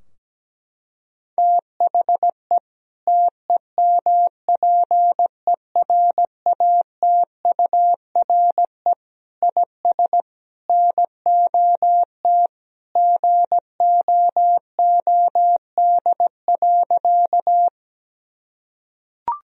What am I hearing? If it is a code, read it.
Morse code, heard as a keyed tone, about 17 wpm, 700 Hz. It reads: THE TEMPERATURE IS NOT GOOD.
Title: Sentences from Top 1000 Words - Review Part 4 - 17wpm